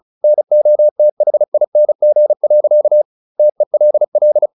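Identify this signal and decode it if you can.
Morse code, heard as a keyed tone, about 34 wpm, 600 Hz. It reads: NOTHING. TELL
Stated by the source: Morse practice at 35 wpm from CW med Gnister, sending Ken Follett's novel Never in English.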